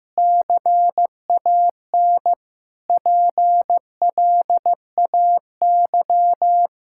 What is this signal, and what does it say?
Morse code, heard as a keyed tone, about 16 wpm, 700 Hz. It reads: CAN PLAY
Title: Sentences from Top 1000 Words - Review Part 5 - 15wpm